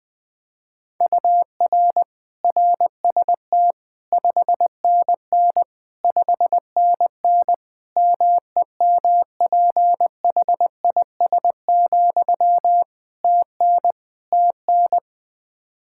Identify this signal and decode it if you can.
Morse code, heard as a keyed tone, about 21 wpm, 700 Hz. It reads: UR RST 5NN 5NN MEMPHIS, TN TN
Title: Practice QSO - Exchange 1 - Beginning - 20wpm